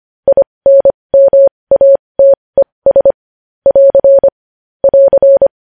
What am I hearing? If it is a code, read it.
Morse code, heard as a keyed tone, about 25 wpm, spaced as for 20 wpm, 550 Hz. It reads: INMATES <AR> <AR>
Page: News Headlines in Morse Code at 20 WPM